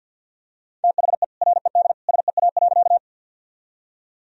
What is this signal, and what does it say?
Morse code, heard as a keyed tone, about 50 wpm, 700 Hz. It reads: THE RED SEA.
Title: Sentences from Top 1000 Words - Review Part 2 - 50wpm